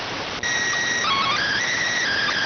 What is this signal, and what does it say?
SSTV (slow-scan television), PD 290.